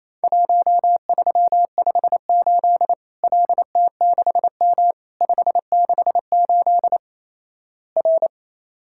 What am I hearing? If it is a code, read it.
Morse code, heard as a keyed tone, about 28 wpm, 700 Hz. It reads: 1358 LT6M 568 R
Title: Worked All Europe DX Contest - QTC Exchange - 28wpm